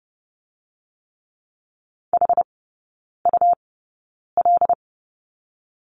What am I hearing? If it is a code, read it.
Morse code, heard as a keyed tone, about 30 wpm, spaced as for 9 wpm, 690 Hz. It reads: HUL
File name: Lesson 17e - Letter u - Three Characters 30wpm